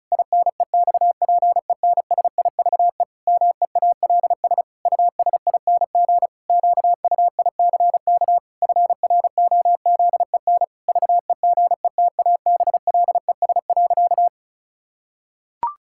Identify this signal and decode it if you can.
Morse code, heard as a keyed tone, about 36 wpm, 700 Hz. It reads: INEXPENSIVE MEALS USING QUICK FROZEN VEGETABLES.
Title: Pangram Challenge v2 - 35wpm